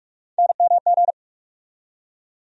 Morse code, heard as a keyed tone, about 47 wpm, 700 Hz. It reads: NMG
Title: USA Call Sign Postfixes - 3 Letters 45wpm